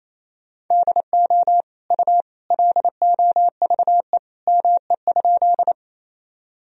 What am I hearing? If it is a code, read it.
Morse code, heard as a keyed tone, about 27 wpm, 700 Hz. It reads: DO U LOVE ME?
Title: Sentences from Top 400 Words and Abbreviations 28wpm